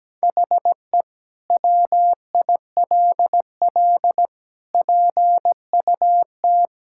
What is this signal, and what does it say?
Morse code, heard as a keyed tone, about 17 wpm, 700 Hz. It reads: HE WILL PUT
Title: Sentences from Top 1000 Words - Review Part 5 - 17wpm